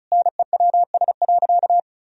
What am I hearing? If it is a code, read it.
Morse code, heard as a keyed tone, about 34 wpm, 700 Hz. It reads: NEWS.
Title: Sentences from Top 1000 Words - Review Part 1 - 35wpm